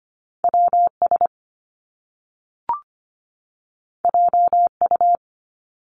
Morse code, heard as a keyed tone, about 25 wpm, 700 Hz. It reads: WS JU
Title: Top Two Letter Combinations in Words 25wpm